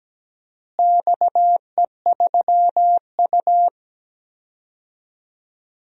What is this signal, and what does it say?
Morse code, heard as a keyed tone, about 17 wpm, 700 Hz. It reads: XE3U
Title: DXCC-100 Call Signs - 4 Characters - Repeated 2x - 17wpm